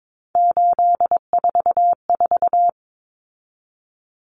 Morse code, heard as a keyed tone, about 22 wpm, 700 Hz. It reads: 844
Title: Common Cut Numbers - 3-Digit 22wpm